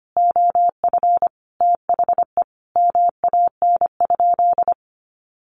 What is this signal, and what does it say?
Morse code, heard as a keyed tone, about 25 wpm, 700 Hz. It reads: OF THE MAN?
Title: Sentences from Top 1500 Words 25wpm